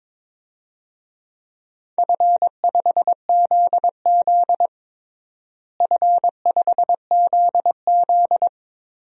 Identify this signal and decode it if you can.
Morse code, heard as a keyed tone, about 22 wpm, 700 Hz. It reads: F5ZZ F5ZZ
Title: DXCC-100 Call Signs - 4 Characters - Repeated 3x - 22wpm